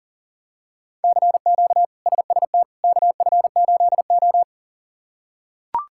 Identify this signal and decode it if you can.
Morse code, heard as a keyed tone, about 40 wpm, 700 Hz. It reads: CQ SST KF8O
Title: K1USN SST QSO Practice - 40wpm